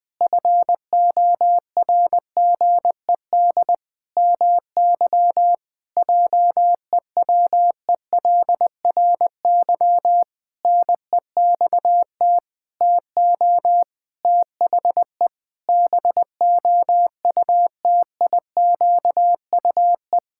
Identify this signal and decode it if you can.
Morse code, heard as a keyed tone, about 20 wpm, 700 Hz. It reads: FORGED MY JEWELRY NEXT TO THE BOUTIQUE
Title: Pangram Challenge v2 - 20wpm